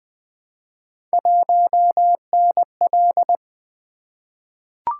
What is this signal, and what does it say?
Morse code, heard as a keyed tone, about 20 wpm, 700 Hz. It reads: 1NL
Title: Three Letter-Number Combinations - 20wpm